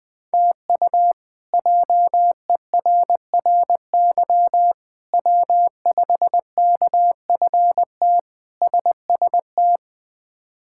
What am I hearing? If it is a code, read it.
Morse code, heard as a keyed tone, about 20 wpm, 700 Hz. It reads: TU JERRY W5KFT SST
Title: K1USN SST QSO Practice - 20wpm